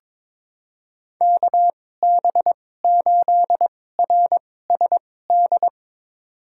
Morse code, heard as a keyed tone, about 21 wpm, spaced as for 16 wpm, 700 Hz. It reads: KB8RSD
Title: USA Call Signs - 6 Characters - 2x Character Spacing - 22wpm